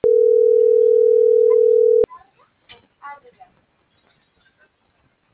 Ringback tone (North America), 440 + 480 Hz.
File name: CENTRAL MABE / out saby.reyes25 - .97143